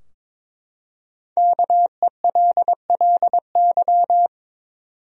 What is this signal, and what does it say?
Morse code, heard as a keyed tone, about 22 wpm, 700 Hz. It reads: KELLY
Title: USA Common Names - 22wpm